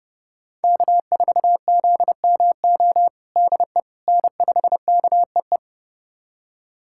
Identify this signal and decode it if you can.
Morse code, heard as a keyed tone, about 32 wpm, 700 Hz. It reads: K4ZMO DE N5KEE
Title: USA Callsign DE Callsign - 5 Characters - 30wpm